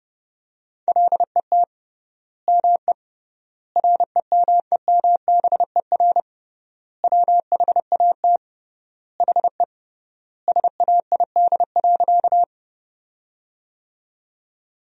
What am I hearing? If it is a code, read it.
Morse code, heard as a keyed tone, about 29 wpm, 700 Hz. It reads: LET ME REMEMBER WHAT HE SAID.
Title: 3x Word-Spacing - Sentences from Top 400 Words - 30wpm